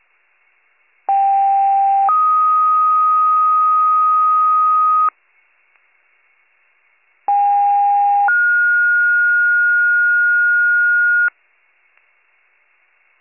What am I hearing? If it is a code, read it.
Two-tone page, heard as a one-second tone, then a three-second tone, about 790 then 1250 Hz, about 810 then 1430 Hz. It is heard from 1.1 to 11.3 s.